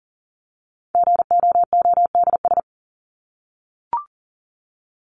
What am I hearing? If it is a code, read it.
Morse code, heard as a keyed tone, about 40 wpm, 700 Hz. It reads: GOODS
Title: ES:S Suffix - Top 100 Words with Base Words 40wpm